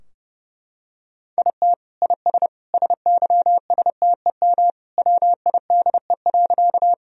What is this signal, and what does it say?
Morse code, heard as a keyed tone, about 32 wpm, 700 Hz. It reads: IT IS SYSTEM WIDE.
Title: Sentences from Top 1000 Words - Review Part 3 - 30wpm